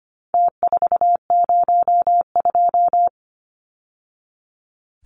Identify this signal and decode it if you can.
Morse code, heard as a keyed tone, about 25 wpm, 700 Hz. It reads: T402
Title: Numbers - Contest Style - 4-Digit 25wpm